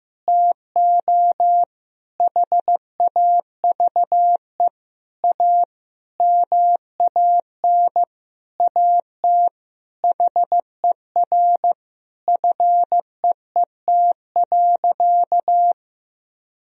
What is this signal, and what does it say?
Morse code, heard as a keyed tone, about 15 wpm, 700 Hz. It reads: TO HAVE A MAN AT HER FEET.
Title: Sentences from Top 1000 Words - Review Part 1 - 15wpm